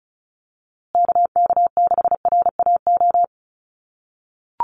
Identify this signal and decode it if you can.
Morse code, heard as a keyed tone, about 34 wpm, 700 Hz. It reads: KK6RAO
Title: USA Call Signs Repeated 3x 35wpm